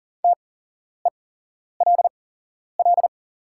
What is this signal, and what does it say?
Morse code, heard as a keyed tone, about 40 wpm, spaced as for 11 wpm, 690 Hz. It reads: TELL
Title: 8x Character Spacing - Top 200 Words - 40wpm